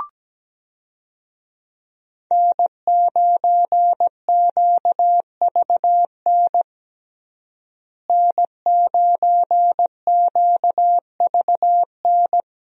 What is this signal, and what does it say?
Morse code, heard as a keyed tone, about 17 wpm, 700 Hz. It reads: N9QVN N9QVN
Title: USA Call Signs - 5 Characters - Repeated 2x - 17wpm